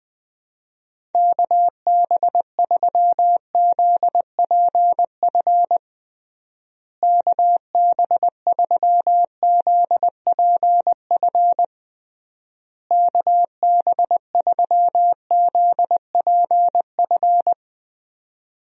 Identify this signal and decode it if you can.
Morse code, heard as a keyed tone, about 21 wpm, 700 Hz. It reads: KB3ZPF KB3ZPF KB3ZPF
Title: USA Call Signs - 6 Characters - Repeated 3x - 20wpm